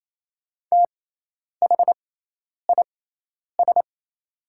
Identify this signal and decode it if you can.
Morse code, heard as a keyed tone, about 28 wpm, spaced as for 10 wpm, 700 Hz. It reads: THIS
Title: 6x Character Spacing - Top 100 Words - 28wpm